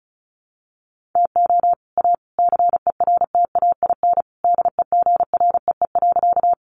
Morse code, heard as a keyed tone, about 34 wpm, 700 Hz. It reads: TO A CERTAIN DEGREE.
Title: Sentences from Top 1500 Words 35wpm